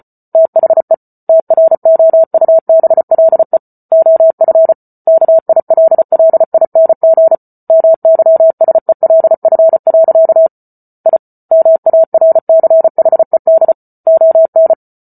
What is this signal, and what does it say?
Morse code, heard as a keyed tone, about 34 wpm, 650 Hz. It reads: THE TROUBLE OF KILLING MYSELF. I MARCHED ON